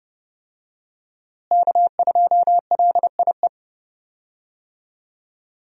Morse code, heard as a keyed tone, about 30 wpm, 700 Hz. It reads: K2LIE
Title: USA Call Signs - 5 Characters - Repeated 2x - 30wpm